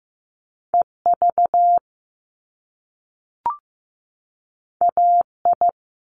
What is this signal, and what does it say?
Morse code, heard as a keyed tone, about 15 wpm, 700 Hz. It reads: EV AI
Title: Top Two Letter Combinations in Words 15wpm